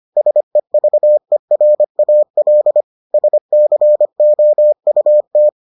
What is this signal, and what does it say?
Morse code, heard as a keyed tone, about 25 wpm, 600 Hz. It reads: SEVERAL SCOUT